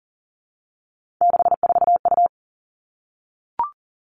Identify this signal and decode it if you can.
Morse code, heard as a keyed tone, about 42 wpm, 700 Hz. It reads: 64U
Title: Three Challenging Letter-Number Combinations 40wpm